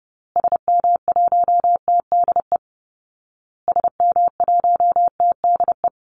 Morse code, heard as a keyed tone, about 30 wpm, 700 Hz. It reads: SM1TDE SM1TDE
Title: Call Signs Repeated 2x - 30wpm